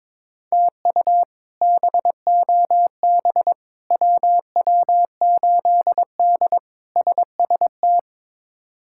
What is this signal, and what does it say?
Morse code, heard as a keyed tone, about 22 wpm, 700 Hz. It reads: TU BOB WW8D SST
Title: K1USN SST QSO Practice - 22wpm